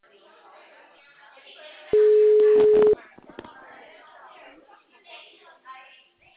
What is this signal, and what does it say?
Ringback tone (Europe, 425 Hz), with speech.